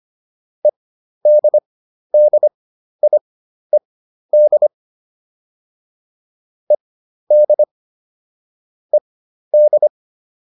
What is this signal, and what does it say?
Morse code, heard as a keyed tone, about 25 wpm, spaced as for 12 wpm, 600 Hz. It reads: EDDIED ED ED
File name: ED6 12